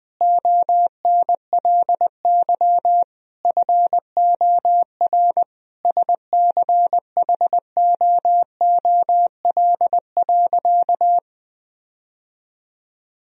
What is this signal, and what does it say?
Morse code, heard as a keyed tone, about 20 wpm, 700 Hz. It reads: ONLY FOR SCHOOL.
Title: Sentences from Top 1000 Words - Review Part 1 - 20wpm